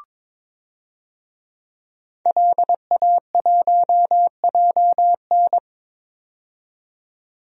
Morse code, heard as a keyed tone, about 21 wpm, 700 Hz. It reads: LA1JN
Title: DXCC-100 Call Signs - 5 Characters - 22wpm